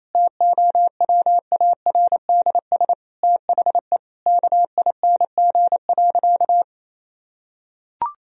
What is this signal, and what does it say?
Morse code, heard as a keyed tone, about 27 wpm, 700 Hz. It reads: TOWARDS THE KING.
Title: Sentences from Top 1000 Words - Review Part 2 - 28wpm